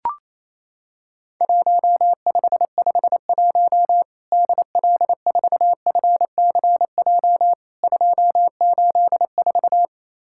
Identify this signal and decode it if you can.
Morse code, heard as a keyed tone, about 28 wpm, 700 Hz. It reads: 1551 DL4FCJ 284
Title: Worked All Europe DX Contest - QTC Exchange - 28wpm